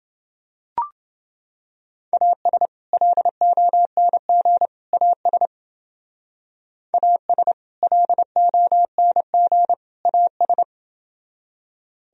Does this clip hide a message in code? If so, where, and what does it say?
Yes: Morse code from 2.1 to 10.6 s, heard as a keyed tone, about 29 wpm, 700 Hz. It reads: AS LONG AS AS LONG AS